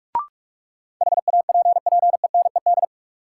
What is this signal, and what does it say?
Morse code, heard as a keyed tone, about 46 wpm, 700 Hz. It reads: HAPPENED
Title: Sentences from Top 1000 Words - Review Part 2 - New Word Drill - 45wpm